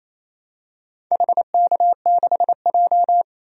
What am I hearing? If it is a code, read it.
Morse code, heard as a keyed tone, about 28 wpm, 700 Hz. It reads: HK6J
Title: DXCC-100 Call Signs - 4 Characters - 28wpm